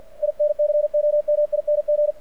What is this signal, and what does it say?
Morse code, heard as a keyed tone, about 50 wpm, 600 Hz, with radio noise. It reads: INCOMING
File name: archived / CRSnet